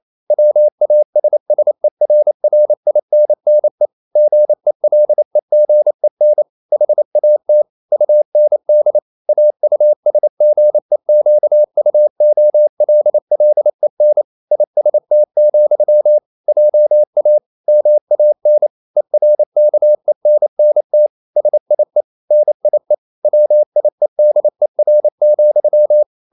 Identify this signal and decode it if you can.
Morse code, heard as a keyed tone, about 28 wpm, 600 Hz. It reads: WASSERRINNE GELEGEN HAT UND AUSGEQUOLLEN IST, JA MAN ERKENNT SIE NIE WIEDER,